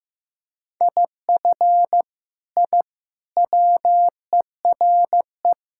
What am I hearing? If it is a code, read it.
Morse code, heard as a keyed tone, about 15 wpm, 700 Hz. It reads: IF I WERE
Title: Sentences from Top 1000 Words - Review Part 6 - 15wpm